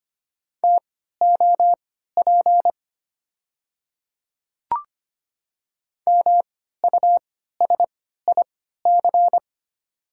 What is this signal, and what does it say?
Morse code, heard as a keyed tone, about 25 wpm, spaced as for 14 wpm, 700 Hz. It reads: TOP MUSIC
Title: 3x Character Spacing - Top 500 Words - 25wpm